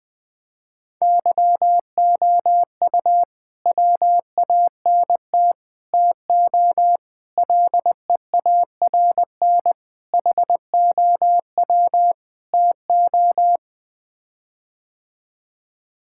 Morse code, heard as a keyed tone, about 20 wpm, 700 Hz. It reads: YOU WANT TO LEARN HOW TO
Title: Sets of 6 Words - 20wpm